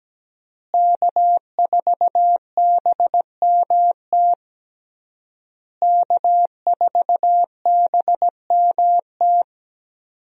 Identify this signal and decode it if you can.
Morse code, heard as a keyed tone, about 17 wpm, 700 Hz. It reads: K4BMT K4BMT
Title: USA Call Signs - 5 Characters - Repeated 3x - 17wpm